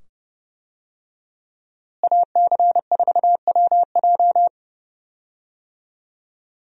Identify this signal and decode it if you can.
Morse code, heard as a keyed tone, about 30 wpm, 700 Hz. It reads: AC4WJ
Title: USA Call Signs - 5 Characters - Repeated 2x - 30wpm